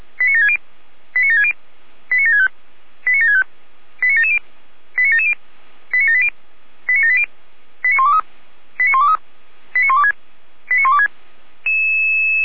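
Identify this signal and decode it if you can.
Five-tone selective call (ZVEI), heard as quick runs of five tones, tones of about 70 ms. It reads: 89760 89760 89765 89765 89770 89770 89780 89780 89123 89123 89126 89126
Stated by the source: ZVEI five-tone call 89755 received by SDR, channel 455.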